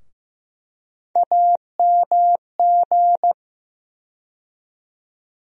Morse code, heard as a keyed tone, about 16 wpm, 700 Hz. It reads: AMG
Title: USA Call Sign Postfixes - 3 Letters 15wpm